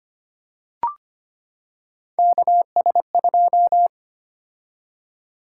Morse code, heard as a keyed tone, about 25 wpm, 700 Hz. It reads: KS2
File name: Three Letter-Number Combinations - 25wpm